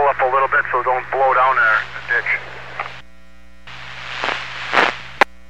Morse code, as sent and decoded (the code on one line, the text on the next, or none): none